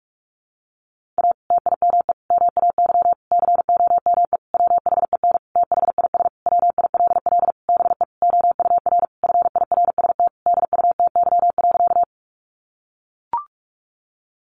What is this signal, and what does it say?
Morse code, heard as a keyed tone, about 44 wpm, 700 Hz. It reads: A TIME MAY COME WHEN THIS WILL BE OUR FIRST DUTY.
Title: Sentences from Top 1500 Words 45wpm